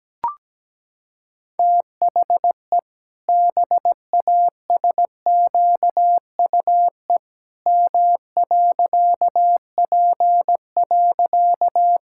Morse code, heard as a keyed tone, about 17 wpm, 700 Hz. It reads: THE BASQUE M.P.
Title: Pangram Challenge v2 - 17wpm